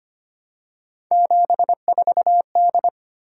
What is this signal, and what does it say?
Morse code, heard as a keyed tone, about 25 wpm, 700 Hz. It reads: 74D
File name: Three Letter-Number Combinations - 25wpm